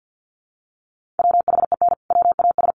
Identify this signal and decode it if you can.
Morse code, heard as a keyed tone, about 50 wpm, 700 Hz. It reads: WHEN WAS
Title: Sentences from Top 100 Words 50wpm